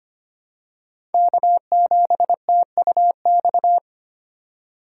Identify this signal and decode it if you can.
Morse code, heard as a keyed tone, about 25 wpm, 700 Hz. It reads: K7TUX
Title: USA Call Signs - 5 Characters - 2x Character Spacing - 25wpm